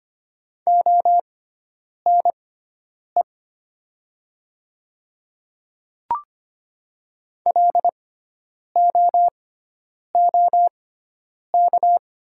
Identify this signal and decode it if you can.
Morse code, heard as a keyed tone, about 25 wpm, spaced as for 9 wpm, 700 Hz. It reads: ONE LOOK
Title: 6x Character Spacing - Top 100 Words - 25wpm